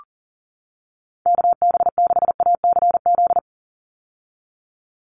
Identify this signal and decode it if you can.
Morse code, heard as a keyed tone, about 40 wpm, 690 Hz. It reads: KB6ACZ
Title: USA Call Signs 40wpm